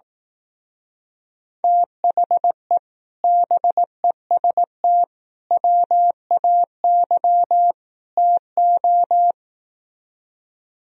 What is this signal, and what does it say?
Morse code, heard as a keyed tone, about 17 wpm, 700 Hz. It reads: THE BEST WAY TO
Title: Speed-Racing - Sets of 4 Words 15wpm